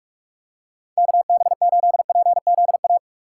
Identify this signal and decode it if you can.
Morse code, heard as a keyed tone, about 46 wpm, 700 Hz. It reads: KB8PZA
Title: USA Call Signs - 6 Characters - 2x Character Spacing - 45wpm